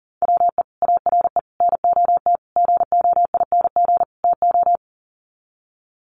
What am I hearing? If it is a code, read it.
Morse code, heard as a keyed tone, about 40 wpm, 700 Hz. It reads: WE ARE NOT GOING TO
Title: Sets of 5 Words 40wpm